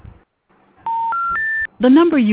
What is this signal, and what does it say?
Special information tone, with speech.